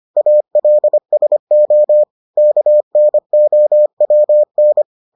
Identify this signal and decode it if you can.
Morse code, heard as a keyed tone, about 25 wpm, 600 Hz. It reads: ALSO KNOWN